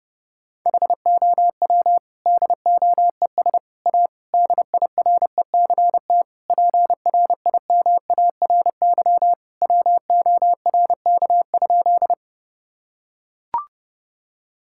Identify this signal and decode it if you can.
Morse code, heard as a keyed tone, about 30 wpm, 700 Hz. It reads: HOW DOES A DIRECT PRIMARY WORK?
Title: Sentences from Top 1000 Words - Review Part 5 - 30wpm